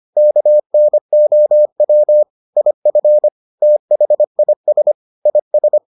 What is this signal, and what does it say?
Morse code, heard as a keyed tone, about 25 wpm, 600 Hz. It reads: KNOW IF THIS IS